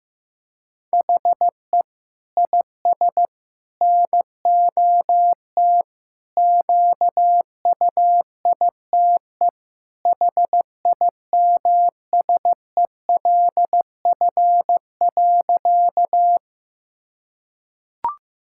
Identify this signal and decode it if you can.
Morse code, heard as a keyed tone, about 15 wpm, 700 Hz. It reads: HE IS NOT QUITE HIMSELF.
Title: Sentences from Top 1600 Words 15wpm